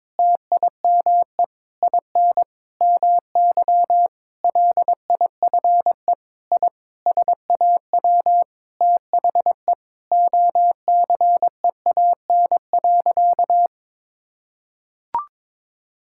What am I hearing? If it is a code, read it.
Morse code, heard as a keyed tone, about 22 wpm, 700 Hz. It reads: TIME IN MY LIFE I SAW THE OCEAN.
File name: practice / 22wpm / Sentences with 8-Word Phrases - 22wpm